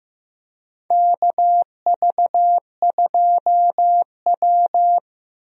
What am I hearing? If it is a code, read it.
Morse code, heard as a keyed tone, about 15 wpm, 700 Hz. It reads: KV2W
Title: USA Call Signs - 4 Characters - Repeated 2x - 15wpm